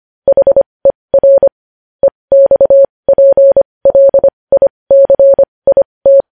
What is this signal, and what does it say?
Morse code, heard as a keyed tone, about 25 wpm, spaced as for 20 wpm, 550 Hz. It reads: HER EXPLICIT
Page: news-headlines-in-morse-code-at-20-wpm